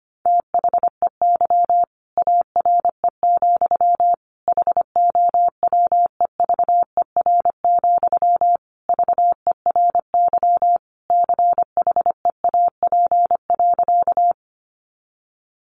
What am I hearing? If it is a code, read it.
Morse code, heard as a keyed tone, about 25 wpm, 700 Hz. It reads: THEY ARE, HOWEVER, VERY CHEAP.
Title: Sentences from Top 1200 Words 25wpm